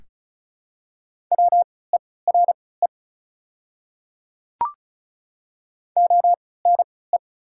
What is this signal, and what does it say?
Morse code, heard as a keyed tone, about 34 wpm, spaced as for 20 wpm, 700 Hz. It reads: WERE ONE
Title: 3x Character Spacing - Top 100 Words - 35wpm